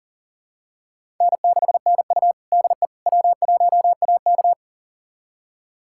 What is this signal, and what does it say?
Morse code, heard as a keyed tone, about 40 wpm, 700 Hz. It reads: N6NU DE W1AK
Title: USA Callsign DE Callsign - 4 Characters - 40wpm